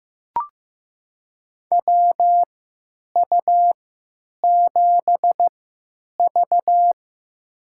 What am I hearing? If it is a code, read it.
Morse code, heard as a keyed tone, about 16 wpm, spaced as for 9 wpm, 700 Hz. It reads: WU7V
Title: USA Call Signs - 4 Characters - 3x Character Spacing - 15wpm